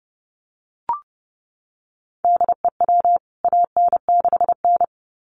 Morse code, heard as a keyed tone, about 30 wpm, 700 Hz. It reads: DEW AN6N
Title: CWT Exchange 30wpm